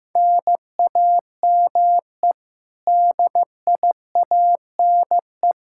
Morse code, heard as a keyed tone, about 15 wpm, 700 Hz. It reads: NAME DIANE
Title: K1USN SST QSO Practice 15wpm